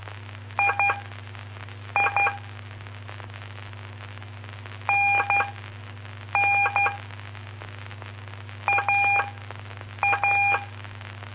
Morse code, heard as a keyed tone, about 12 wpm, spaced as for 7 wpm, 2400 Hz. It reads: II NN AA
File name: pp5bar / aulas de telegrafia / Aula-06